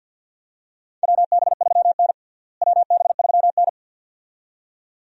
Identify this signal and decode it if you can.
Morse code, heard as a keyed tone, about 51 wpm, 700 Hz. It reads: WB3N WB3N
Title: USA Call Signs - 4 Characters - Repeated 2x - 50wpm